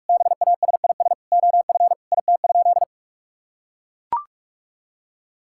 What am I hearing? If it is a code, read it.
Morse code, heard as a keyed tone, about 41 wpm, 700 Hz. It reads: BASIS OF IT?
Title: Sentences from Top 1000 Words - Review Part 5 - 45wpm